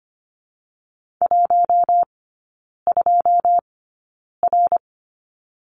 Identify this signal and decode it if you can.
Morse code, heard as a keyed tone, about 25 wpm, spaced as for 9 wpm, 690 Hz. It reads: 12R
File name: Lesson 15e - Number 5 - Three Characters 25wpm